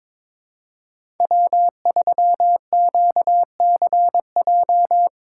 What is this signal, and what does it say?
Morse code, heard as a keyed tone, about 22 wpm, 700 Hz. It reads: W3QCJ
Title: USA Call Signs - 5 Characters - 4x Character Spacing - 22wpm